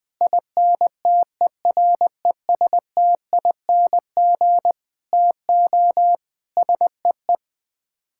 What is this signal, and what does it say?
Morse code, heard as a keyed tone, about 20 wpm, 700 Hz. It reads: INTERESTING TO SEE
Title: Sets of 6 Words - 20wpm